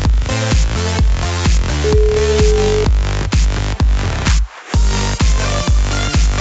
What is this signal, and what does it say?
Ringback tone (Europe, 425 Hz), with noise.